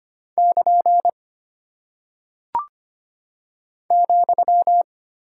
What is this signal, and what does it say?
Morse code, heard as a keyed tone, about 25 wpm, 700 Hz. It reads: <KN> ,
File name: practice / 25wpm / Single Punctuation-Prosign - 25wpm